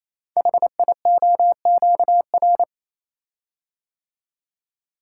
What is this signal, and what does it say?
Morse code, heard as a keyed tone, about 27 wpm, 700 Hz. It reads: HIOQR
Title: Code Groups - 5 Letter:Numbers - 28wpm